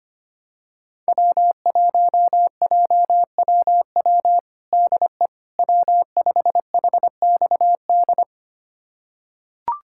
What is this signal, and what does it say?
Morse code, heard as a keyed tone, about 25 wpm, 700 Hz. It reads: W1JWW DE W5HXD
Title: USA Callsign DE Callsign 25wpm